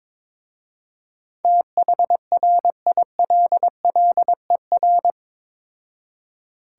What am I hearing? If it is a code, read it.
Morse code, heard as a keyed tone, about 22 wpm, 700 Hz. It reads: THRILLER
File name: practice / 22wpm / Difficult Words with Hints - 22wpm